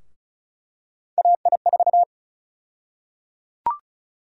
Morse code, heard as a keyed tone, about 36 wpm, 700 Hz. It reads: AI4
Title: USA Call Sign Prefixes - 35wpm